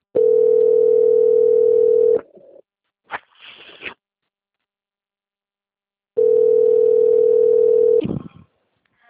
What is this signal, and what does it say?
Ringback tone (North America), 440 + 480 Hz.
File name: out- - - - - .35447